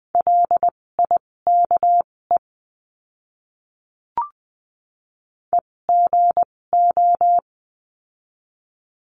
Morse code, heard as a keyed tone, about 20 wpm, spaced as for 16 wpm, 700 Hz. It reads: LIKE EGO
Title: 256 Lesson 42a - Words 12-20wpm